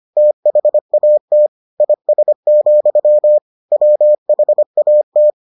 Morse code, heard as a keyed tone, about 25 wpm, 600 Hz. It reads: THAT IS, WHAT